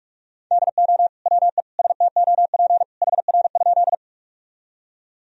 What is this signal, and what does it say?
Morse code, heard as a keyed tone, about 46 wpm, 700 Hz. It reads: DO WE STOP HR?
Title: Sentences from Top 300 Words and Abbreviations 45wpm